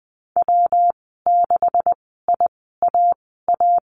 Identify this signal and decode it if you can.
Morse code, heard as a keyed tone, about 20 wpm, spaced as for 15 wpm, 700 Hz. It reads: W6IAA
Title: 167 Lesson 27g - Number 7 - Call Signs 20wpm